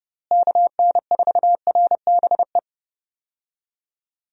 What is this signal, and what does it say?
Morse code, heard as a keyed tone, about 28 wpm, 700 Hz. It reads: KN4RBE
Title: USA Call Signs - 6 Characters - Repeated 3x - 30wpm